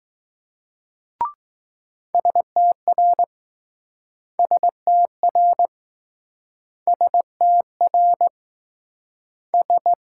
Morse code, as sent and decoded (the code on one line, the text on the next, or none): ... - .-. / ... - .-. / ... - .-. / ...
STR STR STR S